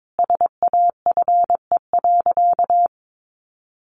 Morse code, heard as a keyed tone, about 22 wpm, 700 Hz. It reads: SAFE.
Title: Sentences from Top 1300 Words 22wpm